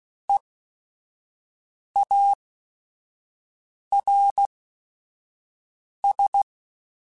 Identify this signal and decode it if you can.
Morse code, heard as a keyed tone, about 16 wpm, spaced as for 5 wpm, 800 Hz. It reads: EARS